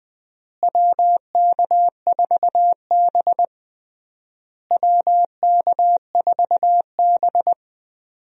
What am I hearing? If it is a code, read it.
Morse code, heard as a keyed tone, about 20 wpm, 700 Hz. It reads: WK4B WK4B
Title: USA Call Signs - 4 Characters - Repeated 3x - 20wpm